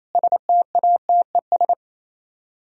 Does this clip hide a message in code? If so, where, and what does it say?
Yes: Morse code from 0.1 to 1.7 s, heard as a keyed tone, about 36 wpm, 700 Hz. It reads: STATES